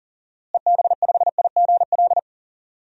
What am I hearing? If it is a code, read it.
Morse code, heard as a keyed tone, about 40 wpm, 700 Hz. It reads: EB5IGL
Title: DXCC-100 Call Signs - 6 Characters - Repeated 3x - 40wpm